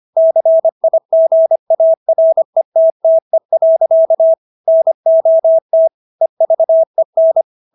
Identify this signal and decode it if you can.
Morse code, heard as a keyed tone, about 25 wpm, 650 Hz. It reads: CIGARETTE. NOT EVEN